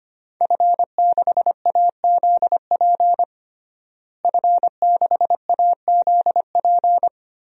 Morse code, heard as a keyed tone, about 25 wpm, 700 Hz. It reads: F6AZP F6AZP
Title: DXCC-100 Call Signs - 5 Characters - Repeated 2x - 25wpm